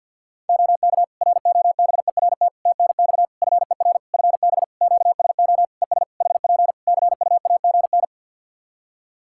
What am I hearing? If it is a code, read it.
Morse code, heard as a keyed tone, about 52 wpm, 700 Hz. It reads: OK ROBERT TNX FER FB QSO ES HP CUAGN
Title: K1USN SST QSO Practice 50wpm